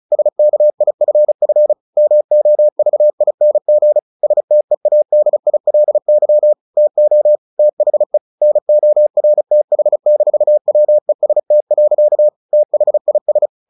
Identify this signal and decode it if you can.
Morse code, heard as a keyed tone, about 34 wpm, 600 Hz. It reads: SKIFF MOVING STEADILY TO THE NORTH-WEST. THIS
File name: the old man and the sea / the old man and the sea 1 35wpm